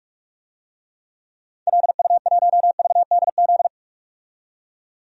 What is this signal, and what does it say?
Morse code, heard as a keyed tone, about 42 wpm, 700 Hz. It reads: LU1VDZ
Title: DXCC-100 Call Signs - 6 Characters - 45wpm